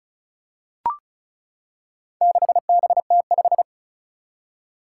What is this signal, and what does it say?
Morse code, heard as a keyed tone, about 36 wpm, 700 Hz. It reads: 6BT5
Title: Four Challenging Letter-Number Combinations 35wpm